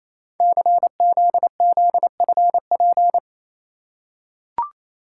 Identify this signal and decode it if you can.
Morse code, heard as a keyed tone, about 28 wpm, 700 Hz. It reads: CZZFP
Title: Code Groups - 5 Letter:Numbers - 28wpm